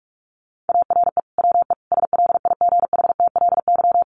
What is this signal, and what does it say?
Morse code, heard as a keyed tone, about 44 wpm, 700 Hz. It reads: ARE WE SLIGHTLY
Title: Sentences from Top 1300 Words 45wpm